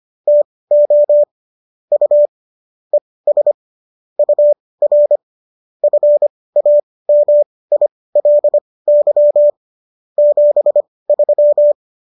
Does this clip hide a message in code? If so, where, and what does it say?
Yes: Morse code from 0.3 to 11.7 s, heard as a keyed tone, about 25 wpm, spaced as for 18 wpm, 590 Hz. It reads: TO U ES UR FAMILY 73